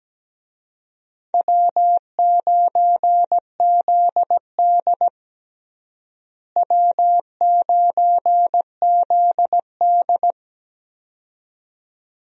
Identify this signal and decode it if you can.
Morse code, heard as a keyed tone, about 17 wpm, 700 Hz. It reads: W9ZD W9ZD
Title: USA Call Signs - 4 Characters - Repeated 2x - 17wpm